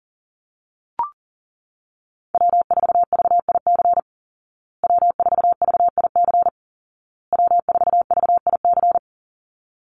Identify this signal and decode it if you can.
Morse code, heard as a keyed tone, about 40 wpm, 700 Hz. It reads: W4VIC W4VIC W4VIC
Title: Call Signs Repeated 2x - 40wpm